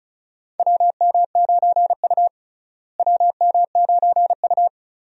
Morse code, heard as a keyed tone, about 34 wpm, 700 Hz. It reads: WM9U WM9U
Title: USA Call Signs - 4 Characters - Repeated 2x - 35wpm